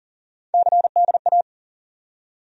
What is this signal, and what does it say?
Morse code, heard as a keyed tone, about 40 wpm, 700 Hz. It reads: CDA
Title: USA Call Sign Postfixes - 3 Letters 40wpm